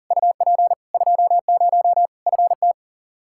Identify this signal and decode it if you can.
Morse code, heard as a keyed tone, about 37 wpm, 700 Hz. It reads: UP 20 FT
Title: Practice QSO - Exchange 2 - Info - 40wpm